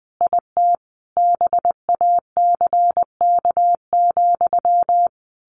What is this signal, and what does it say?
Morse code, heard as a keyed tone, about 20 wpm, 700 Hz. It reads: IT BACK,
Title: A Sound Of Thunder 20wpm